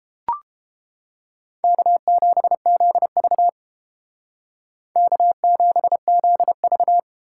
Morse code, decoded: K7ZV K7ZV

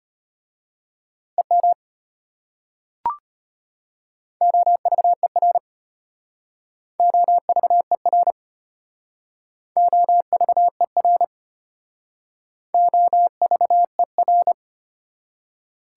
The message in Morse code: EM OVER OVER OVER OVER

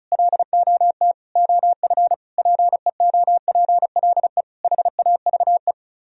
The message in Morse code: LOT OF PEOPLE HAVE

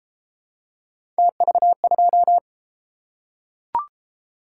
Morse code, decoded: TV2

